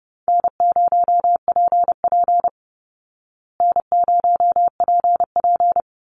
Morse code, heard as a keyed tone, about 30 wpm, 700 Hz. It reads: N0PP N0PP
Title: Easy Call Signs Repeated 2x - 30wpm